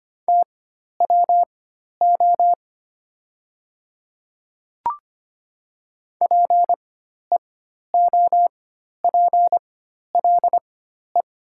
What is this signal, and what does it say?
Morse code, heard as a keyed tone, about 25 wpm, spaced as for 12 wpm, 700 Hz. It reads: TWO PEOPLE
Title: 4x Character Spacing - Top 100 Words - 25wpm